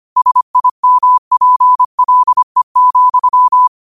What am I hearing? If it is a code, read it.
Morse code, heard as a keyed tone, about 25 wpm, 1000 Hz. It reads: SIMPLE,
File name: qotdfiles / 25wpm